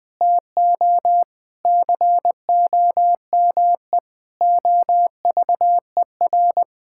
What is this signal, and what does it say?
Morse code, heard as a keyed tone, about 21 wpm, 700 Hz. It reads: TO COME OVER
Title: Sentences with 4-Word Phrases - Spoken Beginning - 20wpm